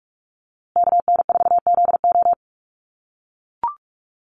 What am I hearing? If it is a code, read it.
Morse code, heard as a keyed tone, about 44 wpm, 700 Hz. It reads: KN4ZO
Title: USA Call Signs Repeated 2x 45wpm